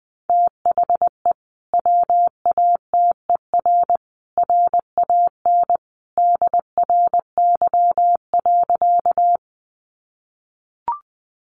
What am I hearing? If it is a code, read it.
Morse code, heard as a keyed tone, about 20 wpm, 700 Hz. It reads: THE WATER RAN DRY.
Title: Sentences from Top 500 Words 20wpm